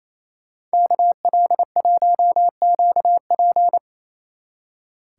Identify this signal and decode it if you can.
Morse code, heard as a keyed tone, about 35 wpm, 700 Hz. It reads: KL1QP